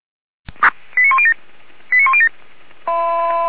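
Five-tone selective call (ZVEI), heard as quick runs of five tones, tones of about 70 ms. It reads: 89107 89107
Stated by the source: ZVEI five-tone selective call 89105 received on SDR.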